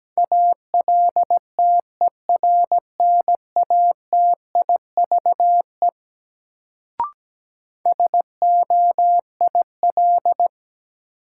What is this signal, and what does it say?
Morse code, heard as a keyed tone, about 17 wpm, 700 Hz. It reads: ALTERNATIVE SOIL
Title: Sentences from Top 1000 Words - Review Part 4 - New Word Drill - 17wpm